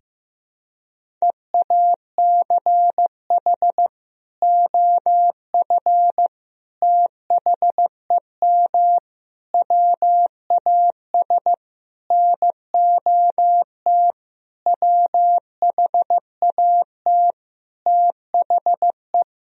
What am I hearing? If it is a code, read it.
Morse code, heard as a keyed tone, about 16 wpm, 700 Hz. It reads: EACH OF THEM WAS NOT WHAT THE
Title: Sentences from Top 1000 Words - Review Part 1 - 15wpm